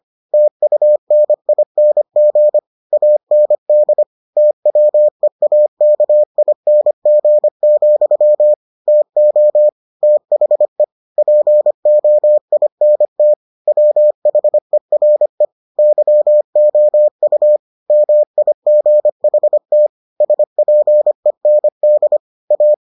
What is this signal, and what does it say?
Morse code, heard as a keyed tone, about 25 wpm, 600 Hz. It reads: TUNING AND TWEAKING, TO THE POINT WHERE YOU MIGHT SPEND A